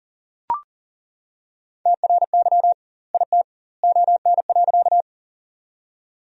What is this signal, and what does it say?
Morse code, heard as a keyed tone, about 40 wpm, 700 Hz. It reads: TRY IT ON.
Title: 2x Word-Spacing - Sentences from Top 200 Words - 40wpm